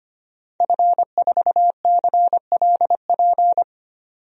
Morse code, heard as a keyed tone, about 25 wpm, 700 Hz. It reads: F4CLP
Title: DXCC-100 Call Signs - 5 Characters - 25wpm